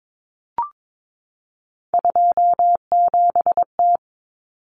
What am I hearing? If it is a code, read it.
Morse code, heard as a keyed tone, about 22 wpm, 700 Hz. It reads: 27T
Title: Common Cut Numbers - 3-Digit 22wpm